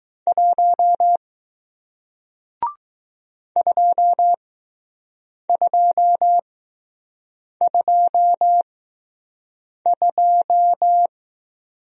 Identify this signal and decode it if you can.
Morse code, heard as a keyed tone, about 20 wpm, spaced as for 6 wpm, 700 Hz. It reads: 1 2222